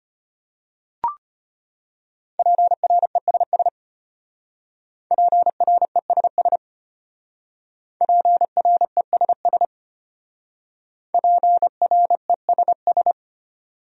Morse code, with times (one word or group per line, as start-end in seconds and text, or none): 2.4-3.7 PRESS
5.1-6.6 PRESS
8.0-9.7 PRESS
11.1-13.1 PRESS